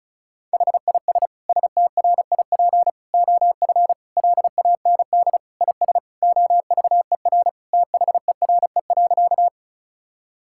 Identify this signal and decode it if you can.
Morse code, heard as a keyed tone, about 36 wpm, 700 Hz. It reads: HIS STRIP OF LAND IS OVER THERE.
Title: Sentences from Top 1600 Words 35wpm